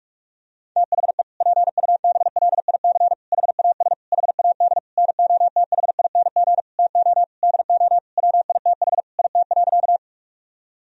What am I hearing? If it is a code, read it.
Morse code, heard as a keyed tone, about 46 wpm, 700 Hz. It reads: THE PUBLIC HAS HAD NOTHING TO DO WITH IT.
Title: Sentences from Top 1000 Words - Review Part 3 - 45wpm